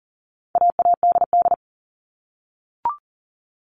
Morse code, heard as a keyed tone, about 40 wpm, 700 Hz. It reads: AADD
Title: All Cut Numbers - 4-Digit 40wpm